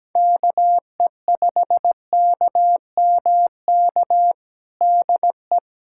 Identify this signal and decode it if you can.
Morse code, heard as a keyed tone, about 17 wpm, 700 Hz. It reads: KE5KMK DE
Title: USA Callsign DE Callsign 17wpm